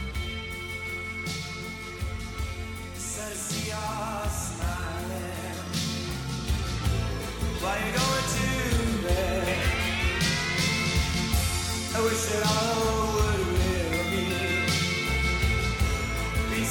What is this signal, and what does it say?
Dial-up modem handshake.
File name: wrct-log hi- 20